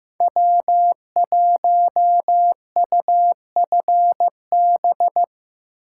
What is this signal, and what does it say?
Morse code, heard as a keyed tone, about 15 wpm, 700 Hz. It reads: W1UFB
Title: USA Call Signs - 5 Characters - 6x Character Spacing - 15wpm